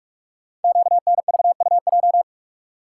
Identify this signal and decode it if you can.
Morse code, heard as a keyed tone, about 44 wpm, 700 Hz. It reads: QNVUJ
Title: Code Groups - 5 Letters - 45wpm